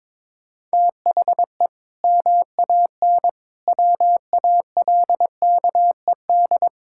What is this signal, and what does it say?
Morse code, heard as a keyed tone, about 22 wpm, 700 Hz. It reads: THE MAN WALKED
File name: practice / 22wpm / Sentences with 2-Word Phrases - 22wpm